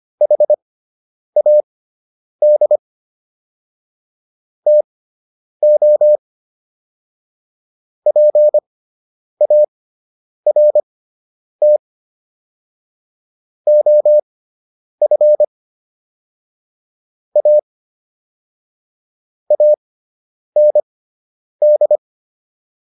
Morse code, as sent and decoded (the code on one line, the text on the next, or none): .... .- -.. / - --- / .--. .- .-. - / --- ..-. / .- / .- -. -..
HAD TO PART OF A AND